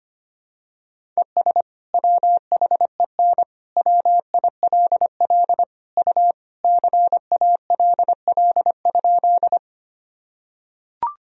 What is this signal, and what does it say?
Morse code, heard as a keyed tone, about 25 wpm, 700 Hz. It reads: ES WHEN WILL U CALL?
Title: Sentences from Top 100 Words and Abbreviations 25wpm